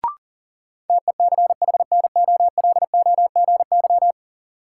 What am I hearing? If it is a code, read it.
Morse code, heard as a keyed tone, about 40 wpm, 700 Hz. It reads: TECHNOLOGY
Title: Sentences from Top 1000 Words - Review Part 5 - New Word Drill - 40wpm